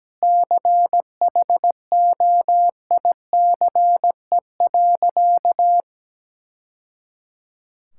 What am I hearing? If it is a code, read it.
Morse code, heard as a keyed tone, about 17 wpm, 700 Hz. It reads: CHOICE.
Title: Sentences from Top 1000 Words - Review Part 6 - 17wpm